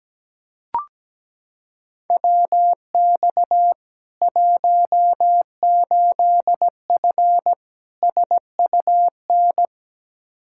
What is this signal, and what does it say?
Morse code, heard as a keyed tone, about 17 wpm, 700 Hz. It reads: WX 18F SUN
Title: Practice QSO - Exchange 2 - Info - 17wpm